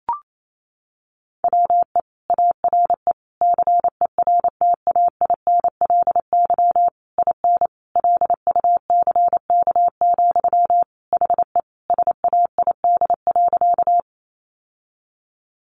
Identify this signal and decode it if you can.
Morse code, heard as a keyed tone, about 28 wpm, 700 Hz. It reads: WE ARE CERTAINLY IN LUCK, HE SAID.
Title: Sentences from Top 1500 Words 28wpm